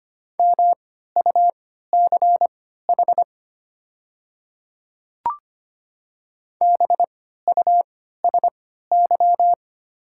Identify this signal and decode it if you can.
Morse code, heard as a keyed tone, about 25 wpm, spaced as for 14 wpm, 700 Hz. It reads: MUCH BUSY
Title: 3x Character Spacing - Top 500 Words - 25wpm